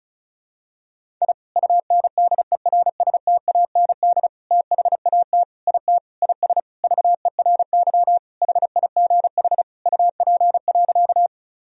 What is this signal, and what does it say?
Morse code, heard as a keyed tone, about 36 wpm, 700 Hz. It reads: I UNDERSTAND THAT IT IS VERY HIGH UP.
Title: Sentences from Top 1000 Words - Review Part 1 - 35wpm